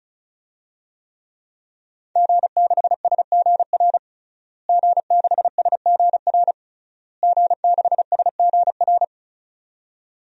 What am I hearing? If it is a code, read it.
Morse code, heard as a keyed tone, about 36 wpm, 700 Hz. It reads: G6SGR G6SGR G6SGR
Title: DXCC-100 Call Signs - 5 Characters - Repeated 3x - 35wpm